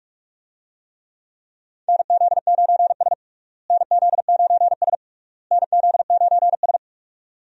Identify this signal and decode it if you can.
Morse code, heard as a keyed tone, about 46 wpm, 700 Hz. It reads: NZ9S NZ9S NZ9S
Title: USA Call Signs - 4 Characters - Repeated 3x - 45wpm